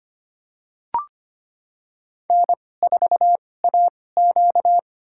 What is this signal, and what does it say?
Morse code, heard as a keyed tone, about 25 wpm, spaced as for 18 wpm, 700 Hz. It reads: N4AQ
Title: USA Call Signs - 4 Characters - 2x Character Spacing - 25wpm